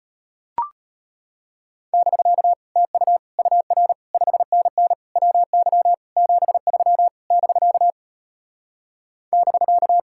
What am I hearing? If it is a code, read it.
Morse code, heard as a keyed tone, about 33 wpm, 700 Hz. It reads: <BK> TU UR 5NN WY 73 <BK> <BK>